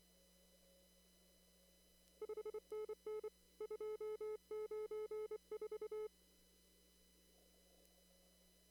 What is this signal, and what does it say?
Morse code, heard as a keyed tone, about 26 wpm, 430 Hz. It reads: 5NN 294
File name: CW / kt5j wpx cw-op-k5pi